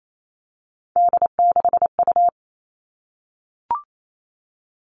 Morse code, heard as a keyed tone, about 28 wpm, 700 Hz. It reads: D6U
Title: All Cut Numbers - 3-Digit 28wpm